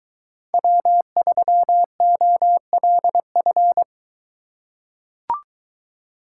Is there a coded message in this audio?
Yes: Morse code from 0.5 to 3.8 s, heard as a keyed tone, about 22 wpm, 700 Hz. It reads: W3OLF